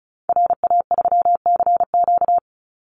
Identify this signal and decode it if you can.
Morse code, heard as a keyed tone, about 34 wpm, 700 Hz. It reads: RA3CQ
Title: Call Signs Repeated 2x - 35wpm